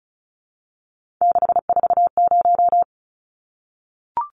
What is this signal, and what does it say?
Morse code, heard as a keyed tone, about 34 wpm, 700 Hz. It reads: 640
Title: Numbers - 3-Digit 35wpm